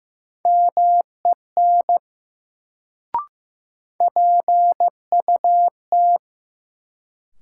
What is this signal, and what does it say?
Morse code, heard as a keyed tone, about 15 wpm, 700 Hz. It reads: MEN PUT
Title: Sentences from Top 200 Words - New Word Drill - 15wpm